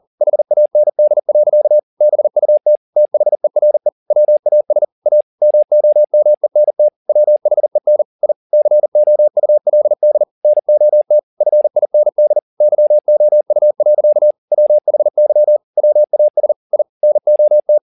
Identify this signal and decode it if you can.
Morse code, heard as a keyed tone, about 40 wpm, 600 Hz. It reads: HAND. BUT THERE WAS A MOMENT WHEN I COULD NOT FIND YOU. WHY WAS I NOT
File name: the old man and the sea 2 40wpm